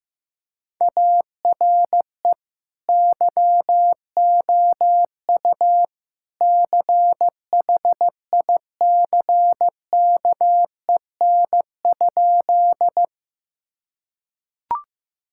Morse code, heard as a keyed tone, about 14 wpm, 700 Hz. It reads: ARE YOU CHICKEN?
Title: Sentences from Top 1000 Words - Review Part 5 - 15wpm